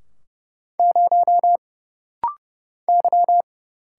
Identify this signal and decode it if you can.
Morse code, heard as a keyed tone, about 31 wpm, 700 Hz. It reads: 0 Y
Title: Single Letter-Number Alt - Rapid-Fire 30wpm